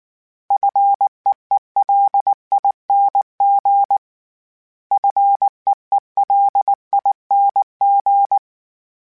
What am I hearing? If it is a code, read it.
Morse code, heard as a keyed tone, about 19 wpm, 800 Hz. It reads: FEELING FEELING
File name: Short Runs / Short Runs-mp3s / SinglePrefixes @18